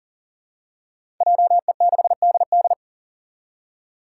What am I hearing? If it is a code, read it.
Morse code, heard as a keyed tone, about 40 wpm, 700 Hz. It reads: JE6DD
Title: Five Challenging Letter-Number Combinations 40wpm